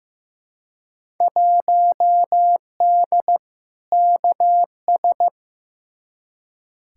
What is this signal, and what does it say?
Morse code, heard as a keyed tone, about 15 wpm, 700 Hz. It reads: 1D KS
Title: Field Day Exchanges - 15wpm